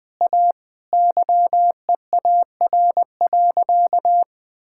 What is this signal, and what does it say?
Morse code, heard as a keyed tone, about 20 wpm, 700 Hz. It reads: A YEAR.
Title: Sentences from Top 1000 Words - Review Part 5 - 20wpm